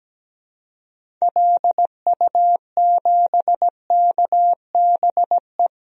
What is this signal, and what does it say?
Morse code, heard as a keyed tone, about 17 wpm, 700 Hz. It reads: LU7KBE
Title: DXCC-100 Call Signs - 6 Characters - 17wpm